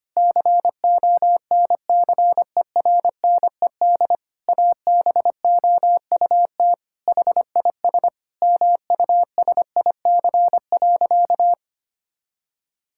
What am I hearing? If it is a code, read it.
Morse code, heard as a keyed tone, about 25 wpm, 700 Hz. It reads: CONCERNED ABOUT HIS MUSIC.
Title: Sentences from Top 1000 Words - Review Part 6 - 25wpm